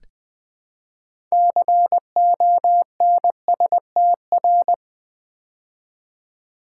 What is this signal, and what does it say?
Morse code, heard as a keyed tone, about 20 wpm, 700 Hz. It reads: CONSTR
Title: Build A Word - 3 Letters At A Time - 9-Letter Words - 20wpm